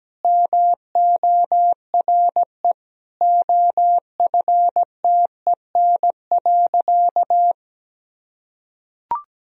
Morse code, heard as a keyed tone, about 17 wpm, 700 Hz. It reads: MORE OFTEN.
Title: Sentences from Top 1000 Words - Review Part 3 - 17wpm